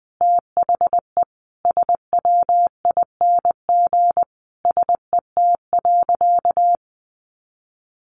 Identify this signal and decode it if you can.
Morse code, heard as a keyed tone, about 20 wpm, 700 Hz. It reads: THE SWING SET.
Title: Additional Easy Phrases 20wpm